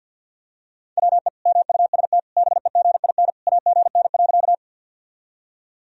Morse code, heard as a keyed tone, about 46 wpm, 700 Hz. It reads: WE MUST BEGIN AGN.
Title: Sentences from Top 300 Words and Abbreviations 50wpm